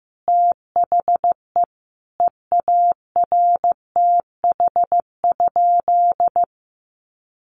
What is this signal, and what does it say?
Morse code, heard as a keyed tone, about 15 wpm, 700 Hz. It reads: THE EARTH?
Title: Sentences from Top 1200 Words 15wpm